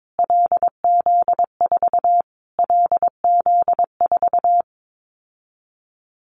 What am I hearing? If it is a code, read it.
Morse code, heard as a keyed tone, about 22 wpm, 700 Hz. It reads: LZ4 LZ4
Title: Callsign Prefixes for Top 25 DXCC - Repeated 3x - 22wpm